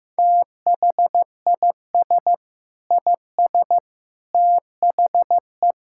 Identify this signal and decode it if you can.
Morse code, heard as a keyed tone, about 16 wpm, 700 Hz. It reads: THIS IS THE
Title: Sentences from Top 1000 Words - Review Part 5 - 15wpm